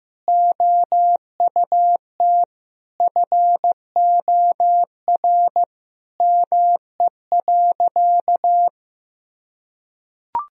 Morse code, heard as a keyed tone, about 15 wpm, 700 Hz. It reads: OUT FOR ME.
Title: Sentences from Top 1000 Words - Review Part 2 - 15wpm